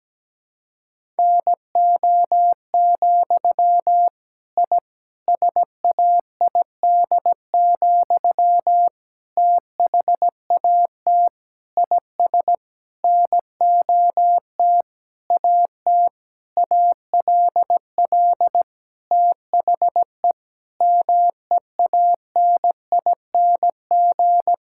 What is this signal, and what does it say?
Morse code, heard as a keyed tone, about 17 wpm, 700 Hz. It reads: NO, I SAID, THAT IS NOT AT ALL THE MEANING
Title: Sentences from Top 1000 Words - Review Part 6 - 17wpm